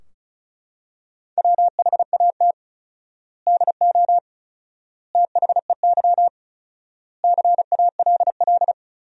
Morse code, heard as a keyed tone, about 34 wpm, 700 Hz. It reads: WHAT DO THEY CALL
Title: 4x Word-Spacing - Sentences from Top 100 Words - 35wpm